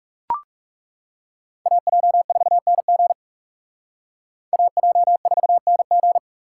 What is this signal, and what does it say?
Morse code, heard as a keyed tone, about 38 wpm, 700 Hz. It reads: AJ4NG AJ4NG